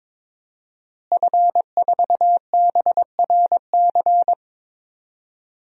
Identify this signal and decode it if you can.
Morse code, heard as a keyed tone, about 22 wpm, 700 Hz. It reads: F4BRC
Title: DXCC-100 Call Signs - 5 Characters - Repeated 2x - 22wpm